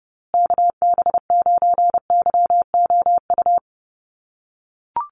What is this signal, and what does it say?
Morse code, heard as a keyed tone, about 29 wpm, 700 Hz. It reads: KB9YOU
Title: USA Call Signs Repeated 2x 30wpm